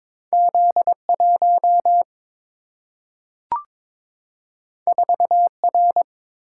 Morse code, heard as a keyed tone, about 22 wpm, 700 Hz. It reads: Z 1 4 R
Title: Single-Letter Single-Number 22wpm